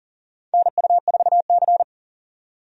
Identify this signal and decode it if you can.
Morse code, heard as a keyed tone, about 40 wpm, 700 Hz. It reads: NU4C
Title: USA Call Signs - 4 Characters - 40wpm